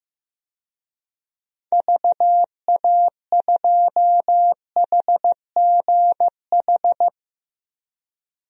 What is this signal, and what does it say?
Morse code, heard as a keyed tone, about 15 wpm, 700 Hz. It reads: VA2HGH
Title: DXCC-100 Call Signs - 6 Characters - Repeated 2x - 15wpm